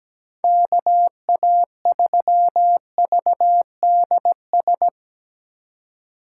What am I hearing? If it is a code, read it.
Morse code, heard as a keyed tone, about 17 wpm, 700 Hz. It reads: KA3VDS
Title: USA Call Signs - 6 Characters - 6x Character Spacing - 17wpm